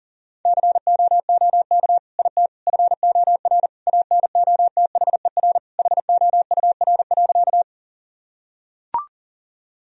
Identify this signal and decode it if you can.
Morse code, heard as a keyed tone, about 40 wpm, 700 Hz. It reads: COOK IT FOR ANOTHER HOUR.
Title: Sentences from Top 1000 Words - Review Part 4 - 40wpm